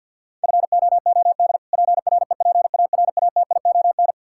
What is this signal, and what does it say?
Morse code, heard as a keyed tone, about 50 wpm, 700 Hz. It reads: FOOD PREPARATION
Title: K1USN SST QSO Practice 50wpm